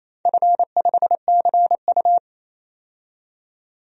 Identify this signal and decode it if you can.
Morse code, heard as a keyed tone, about 28 wpm, 700 Hz. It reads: F5CU
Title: DXCC-100 Call Signs - 4 Characters - 28wpm